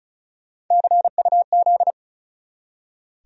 Morse code, heard as a keyed tone, about 43 wpm, 700 Hz. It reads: CUZ
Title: Speed-Racing Abbreviations - 35wpm